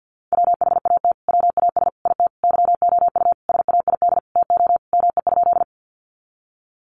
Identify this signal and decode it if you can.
Morse code, heard as a keyed tone, about 50 wpm, 700 Hz. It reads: WHAT WAS IT YOU SAID TO ME?
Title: Sentences from Top 200 Words 50wpm